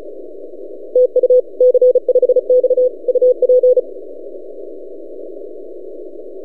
Morse code, heard as a keyed tone, about 35 wpm, 500 Hz. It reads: TU C5X UP
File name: posts / c5x-30cw- -0600z